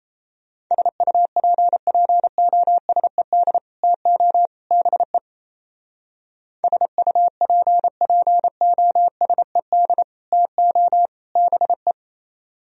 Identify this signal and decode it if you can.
Morse code, heard as a keyed tone, about 32 wpm, 700 Hz. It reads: SUPPOSED TO BE SUPPOSED TO BE
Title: Speed-Racing - Sets of 3 Words 28wpm